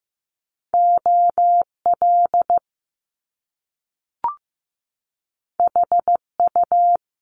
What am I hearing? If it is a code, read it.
Morse code, heard as a keyed tone, about 15 wpm, 700 Hz. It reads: OL HU
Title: Top Two Letter Combinations in Words 15wpm